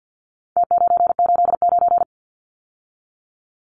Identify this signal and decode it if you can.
Morse code, heard as a keyed tone, about 50 wpm, 700 Hz. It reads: T989
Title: Numbers - Contest Style - 4-Digit 50wpm